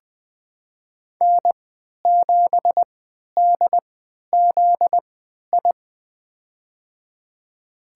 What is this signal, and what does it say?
Morse code, heard as a keyed tone, about 20 wpm, spaced as for 11 wpm, 700 Hz. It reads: N7DZI
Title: USA Call Signs - 5 Characters - 3x Character Spacing - 20wpm